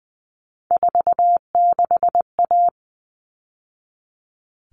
Morse code, heard as a keyed tone, about 20 wpm, 700 Hz. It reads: 46A
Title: All Cut Numbers - 3-Digit 20wpm